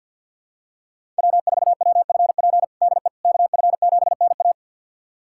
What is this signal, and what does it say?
Morse code, heard as a keyed tone, about 50 wpm, 700 Hz. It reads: W4WFP DE KF7NA
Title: USA Callsign DE Callsign - 5 Characters - 50wpm